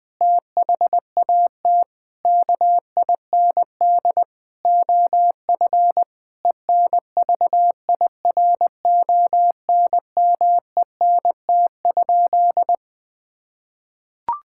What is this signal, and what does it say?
Morse code, heard as a keyed tone, about 21 wpm, 700 Hz. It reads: THAT KIND OF ENVIRONMENT?
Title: Sentences from Top 1000 Words - Review Part 5 - 20wpm